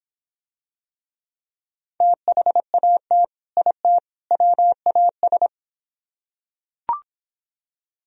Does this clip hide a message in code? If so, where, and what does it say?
Yes: Morse code from 2.0 to 5.5 s, heard as a keyed tone, about 26 wpm, 700 Hz. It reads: THAT IT WAS